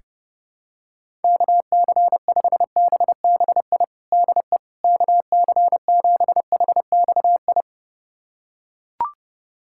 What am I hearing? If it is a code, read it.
Morse code, heard as a keyed tone, about 32 wpm, 700 Hz. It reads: KC5BBI DE KC7HXI
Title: USA Callsign DE Callsign - 6 Characters - 30wpm